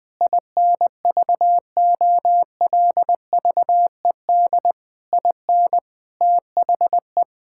Morse code, heard as a keyed tone, about 20 wpm, 700 Hz. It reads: INVOLVED IN THE
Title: Sentences with 6-Word Phrases - 20wpm